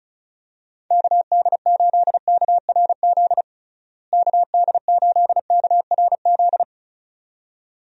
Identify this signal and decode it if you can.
Morse code, heard as a keyed tone, about 36 wpm, 700 Hz. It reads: KD8KRZ KD8KRZ
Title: USA Call Signs - 6 Characters - Repeated 2x - 35wpm